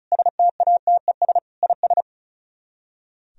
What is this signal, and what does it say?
Morse code, heard as a keyed tone, about 46 wpm, 700 Hz. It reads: STATES IS